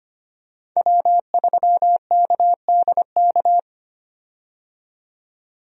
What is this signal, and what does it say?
Morse code, heard as a keyed tone, about 25 wpm, 700 Hz. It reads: W3KDK
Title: USA Call Signs - 5 Characters - 25wpm